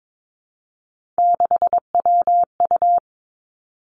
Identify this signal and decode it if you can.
Morse code, heard as a keyed tone, about 22 wpm, 700 Hz. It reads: 6WU
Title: Three Challenging Letter-Number Combinations 22wpm